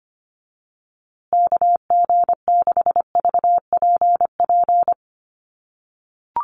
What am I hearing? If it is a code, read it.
Morse code, heard as a keyed tone, about 25 wpm, 700 Hz. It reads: KG6VPP
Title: USA Call Signs Repeated 3x 25wpm